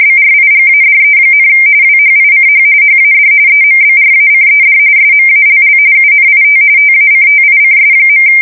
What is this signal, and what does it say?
RTTY, about 45.45 baud. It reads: BRIGADE, WHICH HAD EMBRACED SMALL UNIT TACTICS.
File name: fsk170 / usnavy4